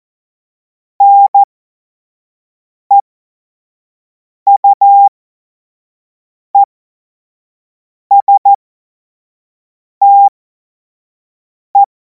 Morse code, heard as a keyed tone, about 14 wpm, spaced as for 5 wpm, 800 Hz. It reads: NEUESTE
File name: Morsekurs / Witze / Witz 09. Juni 2026 25 F70